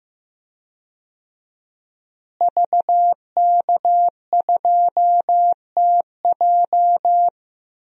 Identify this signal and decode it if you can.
Morse code, heard as a keyed tone, about 15 wpm, 700 Hz. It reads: VK2TJ
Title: DXCC-100 Call Signs - 5 Characters - Repeated 2x - 15wpm